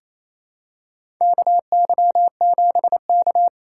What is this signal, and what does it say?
Morse code, heard as a keyed tone, about 28 wpm, 700 Hz. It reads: KY7K
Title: K1USN SST QSO Practice - 28wpm